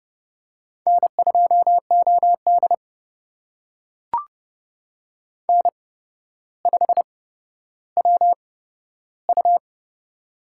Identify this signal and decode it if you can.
Morse code, heard as a keyed tone, about 32 wpm, spaced as for 13 wpm, 700 Hz. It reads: N2OD N5WU
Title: USA Call Signs - 4 Characters - 8x Character Spacing - 30wpm